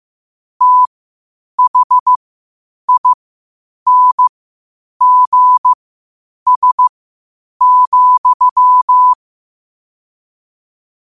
Morse code, heard as a keyed tone, about 15 wpm, spaced as for 8 wpm, 1000 Hz. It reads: THINGS,
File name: qotdfiles / 8wpm